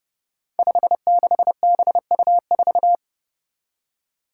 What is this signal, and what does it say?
Morse code, heard as a keyed tone, about 30 wpm, 700 Hz. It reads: 56BU4
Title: Five Challenging Letter-Number Combinations 30wpm